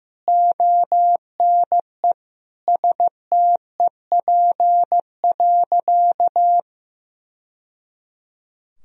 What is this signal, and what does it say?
Morse code, heard as a keyed tone, about 15 wpm, 700 Hz. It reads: ONE STEP.
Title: Sentences from Top 1000 Words - Review Part 2 - 15wpm